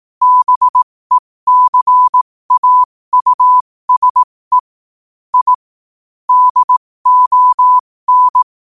Morse code, heard as a keyed tone, about 18 wpm, spaced as for 15 wpm, 1000 Hz. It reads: BECAUSE I DON
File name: qotdfiles / 15wpm